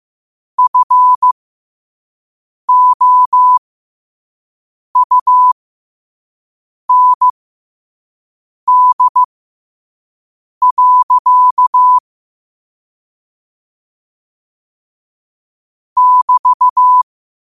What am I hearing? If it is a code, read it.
Morse code, heard as a keyed tone, about 15 wpm, spaced as for 5 wpm, 1000 Hz. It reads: FOUND. <BT>
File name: qotdfiles / 5wpm